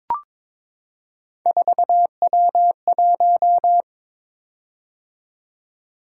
Morse code, heard as a keyed tone, about 22 wpm, 700 Hz. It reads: 4W1
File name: practice / 22wpm / Three Letter-Number Combinations - 22wpm